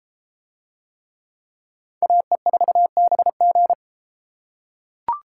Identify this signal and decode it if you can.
Morse code, heard as a keyed tone, about 32 wpm, 700 Hz. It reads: AE4BG